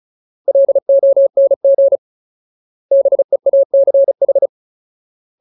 Morse code, heard as a keyed tone, about 34 wpm, 550 Hz. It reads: LONG BEACH
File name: Ultimate Word List / 35WPM Ultimate